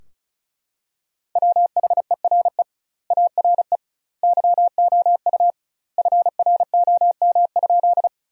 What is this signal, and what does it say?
Morse code, heard as a keyed tone, about 36 wpm, 700 Hz. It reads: WHERE ARE YOU FROM?
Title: 2x Word-Spacing - Sentences from Top 200 Words - 35wpm